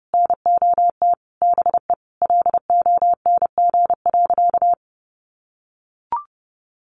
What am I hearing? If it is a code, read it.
Morse code, heard as a keyed tone, about 30 wpm, 700 Hz. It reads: NOT BE LONG.
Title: Sentences from Top 1200 Words 30wpm